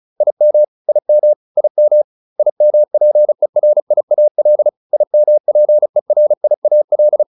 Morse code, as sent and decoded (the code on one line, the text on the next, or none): .. -- / .. -- / .. -- / .. -- .--. .-. .. .- .-.. / .. -- .--. .-. .. .- .-..
IM IM IM IMPERIAL IMPERIAL